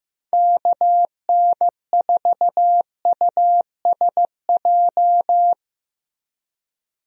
Morse code, heard as a keyed tone, about 15 wpm, 700 Hz. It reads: KN4USJ
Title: USA Call Signs - 6 Characters - Repeated 3x - 15wpm